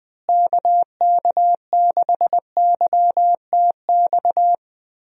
Morse code, heard as a keyed tone, about 20 wpm, 700 Hz. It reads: KK6YTX
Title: USA Call Signs - 6 Characters - Repeated 2x - 20wpm